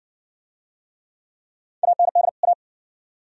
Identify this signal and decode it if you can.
Morse code, heard as a keyed tone, about 60 wpm, 700 Hz. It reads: AND A